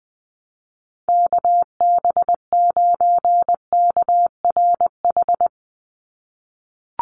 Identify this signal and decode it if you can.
Morse code, heard as a keyed tone, about 20 wpm, 700 Hz. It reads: KB9KRH
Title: USA Call Signs 20wpm